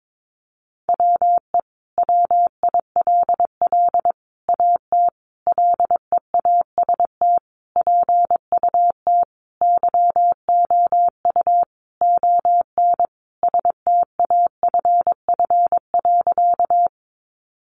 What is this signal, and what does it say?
Morse code, heard as a keyed tone, about 22 wpm, 700 Hz. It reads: WE WILL AT LEAST PUT YOU ON STAFF.
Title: Sentences from Top 1200 Words 22wpm